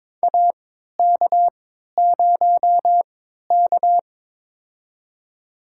Morse code, heard as a keyed tone, about 22 wpm, spaced as for 13 wpm, 700 Hz. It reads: AK0K
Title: USA Call Signs - 4 Characters - 3x Character Spacing - 22wpm